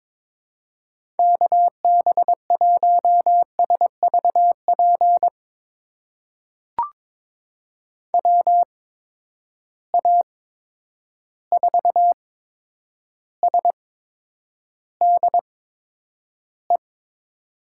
Morse code, heard as a keyed tone, about 22 wpm, spaced as for 9 wpm, 700 Hz. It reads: KB1SVP WA4SDE
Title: USA Call Signs - 6 Characters - 8x Character Spacing - 22wpm